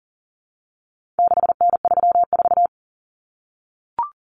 Morse code, heard as a keyed tone, about 40 wpm, 700 Hz. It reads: 6N34
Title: Common Cut Numbers - 4-Digit 40wpm